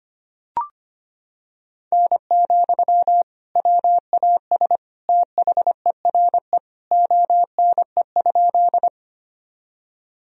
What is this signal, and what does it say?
Morse code, heard as a keyed tone, about 25 wpm, 700 Hz. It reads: N, WAS THERE ONE?
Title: Sentences from Top 100 Words and Abbreviations 25wpm